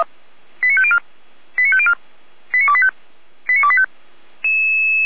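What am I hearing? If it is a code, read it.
Five-tone selective call (ZVEI), heard as quick runs of five tones, tones of about 70 ms. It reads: 89504 89504 89285 89285
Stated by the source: ZVEI five-tone selective call 89025 received on SDR.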